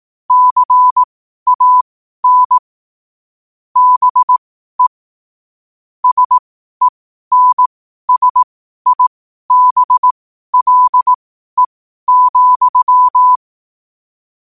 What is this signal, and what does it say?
Morse code, heard as a keyed tone, about 18 wpm, spaced as for 12 wpm, 1000 Hz. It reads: CAN BE SENSIBLE,
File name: qotdfiles / 12wpm